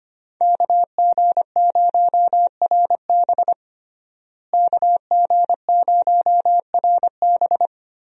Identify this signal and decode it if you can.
Morse code, heard as a keyed tone, about 25 wpm, 700 Hz. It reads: KG0RB KG0RB
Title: USA Call Signs - 5 Characters - Repeated 3x - 25wpm